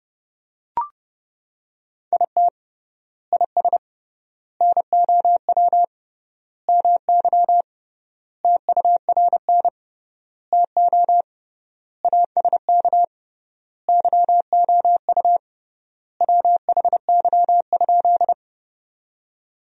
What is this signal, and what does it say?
Morse code, heard as a keyed tone, about 30 wpm, 700 Hz. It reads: IT IS NOW MY TURN TO ASK YOU WHY?
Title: 3x Word-Spacing - Sentences from Top 200 Words - 30wpm